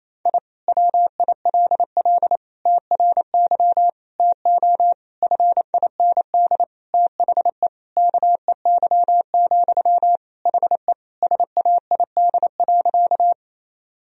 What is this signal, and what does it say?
Morse code, heard as a keyed tone, about 27 wpm, 700 Hz. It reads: I WILL TRY TO FIND THE KEY, HE SAID.
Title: Sentences from Top 1000 Words - Review Part 4 - 28wpm